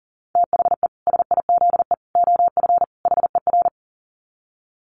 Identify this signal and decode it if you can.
Morse code, heard as a keyed tone, about 40 wpm, 700 Hz. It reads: THE SIZE OF HER
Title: Dit Challenge 40wpm